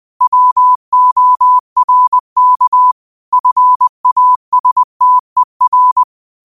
Morse code, heard as a keyed tone, about 20 wpm, 1000 Hz. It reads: WORK FASTER